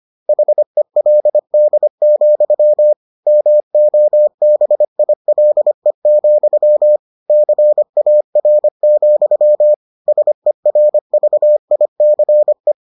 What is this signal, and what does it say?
Morse code, heard as a keyed tone, about 25 wpm, 600 Hz. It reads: HELD, MOBILE, CAR, SERVICE